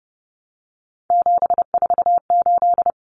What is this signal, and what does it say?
Morse code, heard as a keyed tone, about 30 wpm, 700 Hz. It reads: 748
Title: Common Cut Numbers - 3-Digit 30wpm